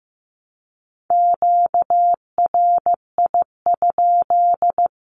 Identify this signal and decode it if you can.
Morse code, heard as a keyed tone, about 15 wpm, 700 Hz. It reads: QRI?
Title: Q-Codes Repeated 3x 15wpm